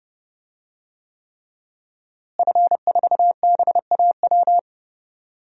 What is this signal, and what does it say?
Morse code, heard as a keyed tone, about 32 wpm, 700 Hz. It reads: F4BAW
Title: DXCC-100 Call Signs - 5 Characters - Repeated 3x - 30wpm